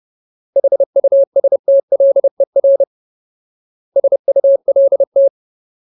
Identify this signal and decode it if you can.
Morse code, heard as a keyed tone, about 30 wpm, 550 Hz. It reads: HUSTLER SULT
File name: Ultimate Word List / 30WPM Ultimate